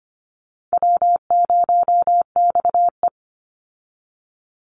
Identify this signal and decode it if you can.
Morse code, heard as a keyed tone, about 25 wpm, 700 Hz. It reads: W0XE
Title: Easy Call Signs Repeated 3x - 25wpm